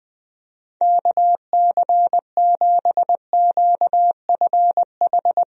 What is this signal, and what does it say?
Morse code, heard as a keyed tone, about 20 wpm, 700 Hz. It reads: KC7QFH
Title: USA Call Signs - 6 Characters - Repeated 3x - 20wpm